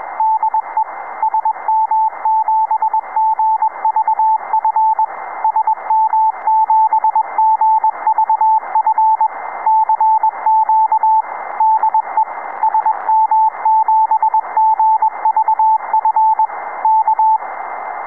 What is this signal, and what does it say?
Morse code, heard as a keyed tone, about 21 wpm, 880 Hz, with radio noise. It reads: DE SM7GVF SM7GVF CQ DE SM7GVF K
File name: sm7gvf CQ